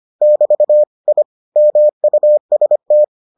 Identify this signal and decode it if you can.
Morse code, heard as a keyed tone, about 25 wpm, 600 Hz. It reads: <BT> I MUST